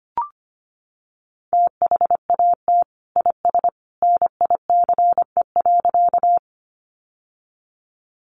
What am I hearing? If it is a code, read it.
Morse code, heard as a keyed tone, about 25 wpm, 700 Hz. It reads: THAT IS NICE.
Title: Sentences from Top 1100 Words 25wpm